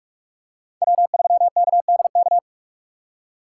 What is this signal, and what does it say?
Morse code, heard as a keyed tone, about 46 wpm, 700 Hz. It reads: W3KDK
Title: USA Call Signs - 5 Characters - 3x Character Spacing - 45wpm